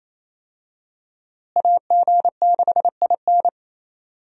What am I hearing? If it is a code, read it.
Morse code, heard as a keyed tone, about 28 wpm, 700 Hz. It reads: AG6IN
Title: USA Call Signs - 5 Characters - Repeated 3x - 28wpm